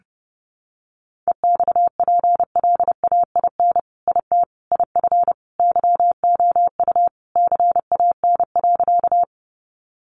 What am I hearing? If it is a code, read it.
Morse code, heard as a keyed tone, about 30 wpm, 700 Hz. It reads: EXPLAIN IT IF YOU CAN.
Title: Sentences from Top 1100 Words 30wpm